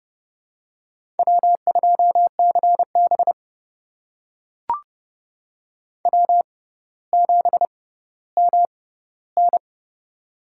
Morse code, heard as a keyed tone, about 30 wpm, spaced as for 15 wpm, 700 Hz. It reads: W2CB W7MN
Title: USA Call Signs - 6x Character Spacing - 30wpm